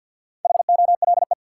Morse code, heard as a keyed tone, about 52 wpm, 700 Hz. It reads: HOLE
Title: Four Letter Words - 50wpm